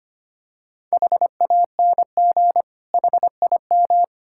Morse code, heard as a keyed tone, about 25 wpm, 700 Hz. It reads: HANG HIM
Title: Sentences from Top 1600 Words 25wpm